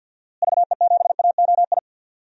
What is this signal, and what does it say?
Morse code, heard as a keyed tone, about 52 wpm, 700 Hz. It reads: VE7AOI
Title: DXCC-100 Call Signs - 6 Characters - Repeated 3x - 50wpm